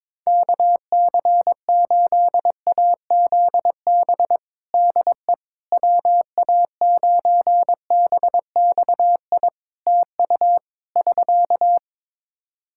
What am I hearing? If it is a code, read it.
Morse code, heard as a keyed tone, about 22 wpm, 700 Hz. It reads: KC8AZB DE WA9BXI TU <SK>
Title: K1USN SST QSO Practice 22wpm